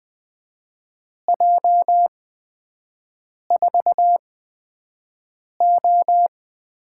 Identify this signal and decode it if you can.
Morse code, heard as a keyed tone, about 20 wpm, spaced as for 5 wpm, 700 Hz. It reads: J4O
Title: USA Call Signs - 8x Character Spacing - 20wpm